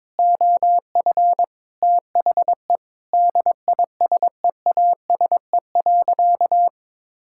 Morse code, heard as a keyed tone, about 22 wpm, 700 Hz. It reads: OF THE DISEASE.
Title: Sentences from Top 1000 Words - Review Part 6 - 22wpm